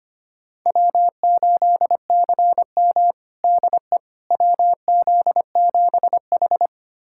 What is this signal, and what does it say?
Morse code, heard as a keyed tone, about 25 wpm, 700 Hz. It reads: W8CM DE WZ7H
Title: USA Callsign DE Callsign - 4 Characters - 25wpm